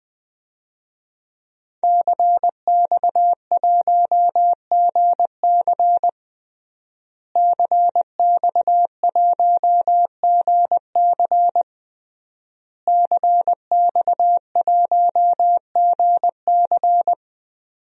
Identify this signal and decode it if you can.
Morse code, heard as a keyed tone, about 20 wpm, 700 Hz. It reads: CX1GC CX1GC CX1GC
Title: DXCC-100 Call Signs - 5 Characters - Repeated 3x - 20wpm